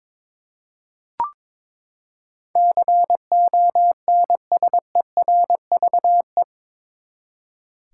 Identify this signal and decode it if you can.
Morse code, heard as a keyed tone, about 22 wpm, 700 Hz. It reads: CONSERVE
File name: practice / 22wpm / Eight Letter Words - 22wpm